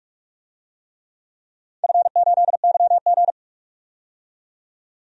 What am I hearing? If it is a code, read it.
Morse code, heard as a keyed tone, about 46 wpm, 700 Hz. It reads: F8YG
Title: DXCC-100 Call Signs - 4 Characters - 45wpm